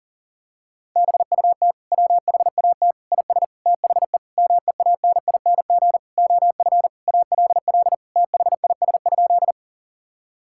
Morse code, heard as a keyed tone, about 40 wpm, 700 Hz. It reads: BUT WHAT IS THE MEANING OF ALL THIS?
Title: Sentences from Top 1000 Words - Review Part 6 - 40wpm